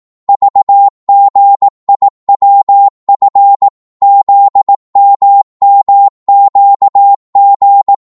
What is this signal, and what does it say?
Morse code, heard as a keyed tone, about 18 wpm, 800 Hz. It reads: VGIWF ZMMQG